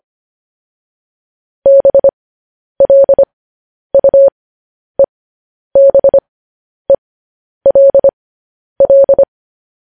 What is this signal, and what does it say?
Morse code, heard as a keyed tone, about 25 wpm, spaced as for 10 wpm, 550 Hz. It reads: BLUEBELL
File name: podcast / News headlines-1x-25x10